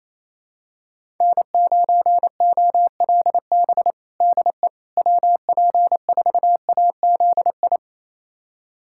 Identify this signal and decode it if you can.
Morse code, heard as a keyed tone, about 27 wpm, 700 Hz. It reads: N9OLB DE WP4AZI
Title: USA Callsign DE Callsign 28wpm